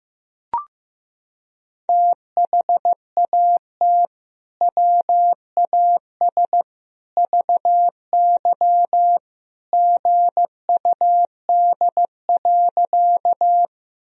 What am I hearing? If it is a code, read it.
Morse code, heard as a keyed tone, about 15 wpm, 700 Hz. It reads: THAT WAS VY GUD.
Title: Sentences from Top 200 Words and Abbreviations 15wpm